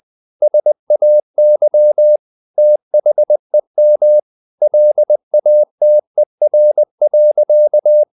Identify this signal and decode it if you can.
Morse code, heard as a keyed tone, about 20 wpm, 600 Hz. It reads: SAY THEM LATER.
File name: the old man and the sea part 2 20wpm